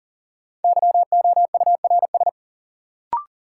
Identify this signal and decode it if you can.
Morse code, heard as a keyed tone, about 40 wpm, 700 Hz. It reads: YOURS
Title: Sentences from Top 1500 Words - New Word Drill - 40wpm